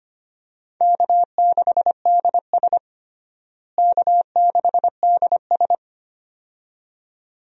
Morse code, heard as a keyed tone, about 25 wpm, 700 Hz. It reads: K6DS K6DS
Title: USA Call Signs - 4 Characters - Repeated 2x - 25wpm